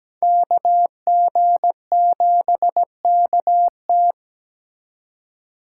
Morse code, heard as a keyed tone, about 20 wpm, 700 Hz. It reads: KG7KT
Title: Speed Racing - USA 2x2 Call Signs 17wpm